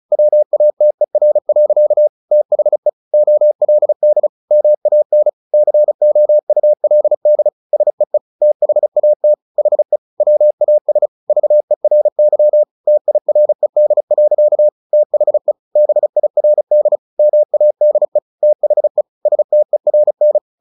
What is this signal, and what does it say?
Morse code, heard as a keyed tone, about 34 wpm, 600 Hz. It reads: WATER. THE OLD MAN COULD SEE THAT HE WAS VERY TIRED. THE BIRD MADE THE STERN
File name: the old man and the sea / the old man and the sea 1 35wpm